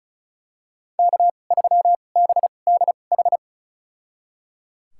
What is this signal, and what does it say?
Morse code, heard as a keyed tone, about 34 wpm, spaced as for 25 wpm, 700 Hz. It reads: K3BDH
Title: USA Call Signs - 2x Character Spacing - 35wpm